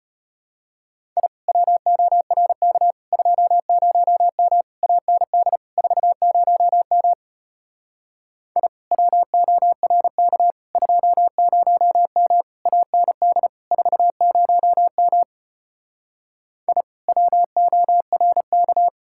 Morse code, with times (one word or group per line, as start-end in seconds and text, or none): 1.2-1.3 I
1.5-2.9 WORK
3.1-4.6 20M
4.8-5.6 AND
5.8-7.1 40M
8.6-8.7 I
8.9-10.5 WORK
10.8-12.4 20M
12.7-13.5 AND
13.7-15.2 40M
16.7-16.8 I
17.1-18.9 WORK